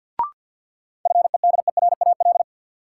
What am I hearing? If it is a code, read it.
Morse code, heard as a keyed tone, about 53 wpm, 700 Hz. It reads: FEDERAL
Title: Sets of 3 Words - New Word Drill - 50wpm